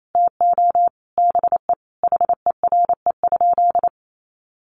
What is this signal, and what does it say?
Morse code, heard as a keyed tone, about 28 wpm, 700 Hz. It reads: TO BE HERE?
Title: Sentences from Top 1100 Words 28wpm